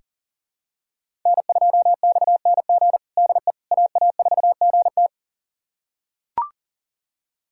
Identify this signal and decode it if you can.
Morse code, heard as a keyed tone, about 40 wpm, 700 Hz. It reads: N2XNG DE AA4GT
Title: USA Callsign DE Callsign - 5 Characters - 40wpm